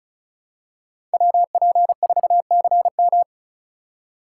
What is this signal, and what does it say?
Morse code, heard as a keyed tone, about 36 wpm, 700 Hz. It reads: WP4CM
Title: USA Call Signs - 3x Character Spacing - 35wpm